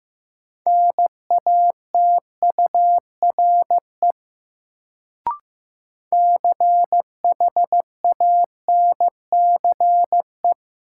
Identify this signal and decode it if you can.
Morse code, heard as a keyed tone, about 15 wpm, 700 Hz. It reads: NATURE CHANCE
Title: Sets of 3 Words - New Word Drill - 15wpm